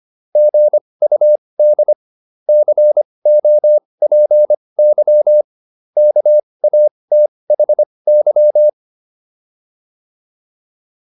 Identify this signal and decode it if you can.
Morse code, heard as a keyed tone, about 25 wpm, spaced as for 20 wpm, 600 Hz. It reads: GUD COPY KATHY